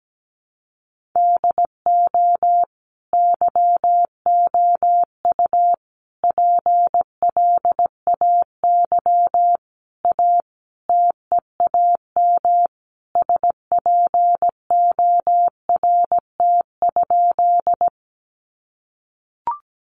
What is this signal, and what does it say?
Morse code, heard as a keyed tone, about 17 wpm, 700 Hz. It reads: DO YOU PLAY A TEAM SPORT?
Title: Sentences from Top 1500 Words 17wpm